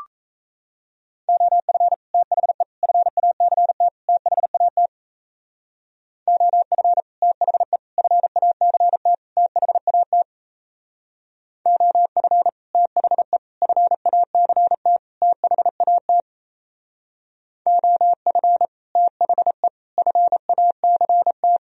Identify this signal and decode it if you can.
Morse code, heard as a keyed tone, about 37 wpm, 700 Hz. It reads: OF THE FACT THAT OF THE FACT THAT OF THE FACT THAT OF THE FACT